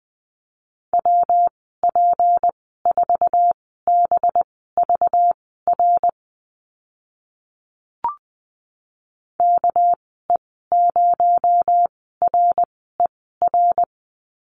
Morse code, heard as a keyed tone, about 20 wpm, spaced as for 14 wpm, 700 Hz. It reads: WP4BVR KE0RER
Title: 238 Lesson 38g - Letter q - Call Signs 20wpm